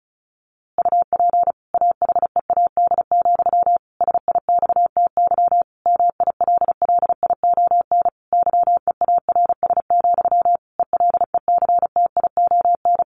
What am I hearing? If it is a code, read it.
Morse code, heard as a keyed tone, about 34 wpm, 700 Hz. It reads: UP AHEAD, SIXTY MILLION YEARS, ELECTION
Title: A Sound Of Thunder 35wpm